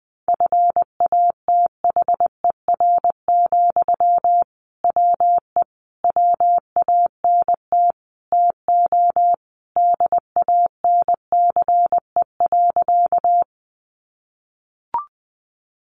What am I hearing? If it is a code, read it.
Morse code, heard as a keyed tone, about 20 wpm, 700 Hz. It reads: FATHER, WE WANT TO DANCE.
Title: Sentences from Top 700 Words 20wpm